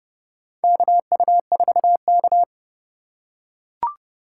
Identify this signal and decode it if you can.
Morse code, heard as a keyed tone, about 30 wpm, 700 Hz. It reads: KU4K
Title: USA Call Signs - 4 Characters - 3x Character Spacing - 30wpm